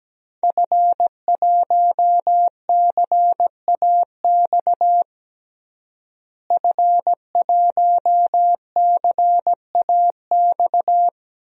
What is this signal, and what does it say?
Morse code, heard as a keyed tone, about 17 wpm, 700 Hz. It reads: F1CAX F1CAX
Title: DXCC-100 Call Signs - 5 Characters - Repeated 2x - 17wpm